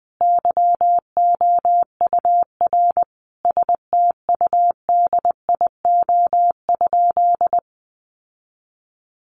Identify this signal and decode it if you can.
Morse code, heard as a keyed tone, about 20 wpm, 700 Hz. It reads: YOUR STUDIO?
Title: Sentences from Top 1200 Words 20wpm